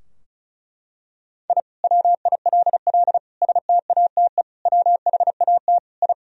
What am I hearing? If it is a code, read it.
Morse code, heard as a keyed tone, about 34 wpm, 700 Hz. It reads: I WILL STATE WHAT I
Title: Sentences from Top 1000 Words - Review Part 2 - 35wpm